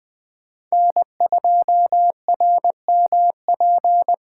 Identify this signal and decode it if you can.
Morse code, heard as a keyed tone, about 20 wpm, 700 Hz. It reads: N2RMP
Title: USA Call Signs - 6x Character Spacing - 20wpm